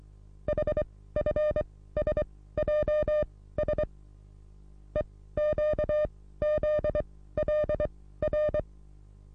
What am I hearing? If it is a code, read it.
Morse code, heard as a keyed tone, about 27 wpm, spaced as for 16 wpm, 600 Hz. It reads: HFSJS EQZLR
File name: repos / Morsekurs / DJ4UF Morsen27